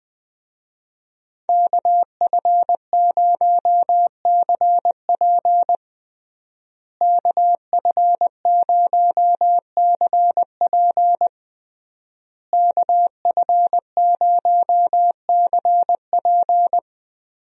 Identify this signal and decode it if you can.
Morse code, heard as a keyed tone, about 21 wpm, 700 Hz. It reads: KF0CP KF0CP KF0CP
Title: USA Call Signs - 5 Characters - Repeated 3x - 20wpm